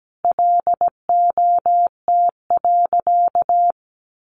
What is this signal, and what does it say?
Morse code, heard as a keyed tone, about 17 wpm, 700 Hz. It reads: LOT.
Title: Sentences from Top 1500 Words 17wpm